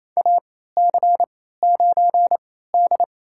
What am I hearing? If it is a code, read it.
Morse code, heard as a keyed tone, about 29 wpm, spaced as for 16 wpm, 700 Hz. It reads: AC9D
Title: USA Call Signs - 4 Characters - 3x Character Spacing - 28wpm